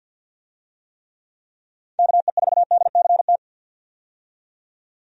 Morse code, heard as a keyed tone, about 51 wpm, 700 Hz. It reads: KE4DCT
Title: USA Call Signs - 6 Characters - 50wpm